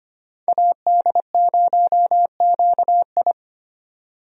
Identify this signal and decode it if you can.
Morse code, heard as a keyed tone, about 25 wpm, 700 Hz. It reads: AD0QI
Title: USA Call Signs - 5 Characters - 25wpm